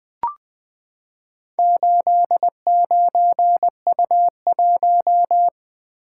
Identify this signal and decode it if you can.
Morse code, heard as a keyed tone, about 20 wpm, 700 Hz. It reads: 89U1
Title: Four Letter-Number Combinations - 20wpm